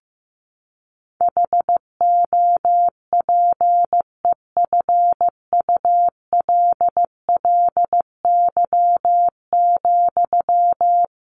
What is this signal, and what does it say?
Morse code, heard as a keyed tone, about 15 wpm, 700 Hz. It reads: HOPEFULLY,
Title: Sentences from Top 1500 Words 15wpm